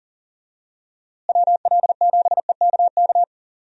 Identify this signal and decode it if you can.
Morse code, heard as a keyed tone, about 40 wpm, 700 Hz. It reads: WL7EKK
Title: USA Call Signs - 6 Characters - Repeated 3x - 40wpm